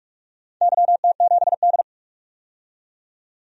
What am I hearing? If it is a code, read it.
Morse code, heard as a keyed tone, about 46 wpm, 700 Hz. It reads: YT7D
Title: DXCC-100 Call Signs - 4 Characters - 45wpm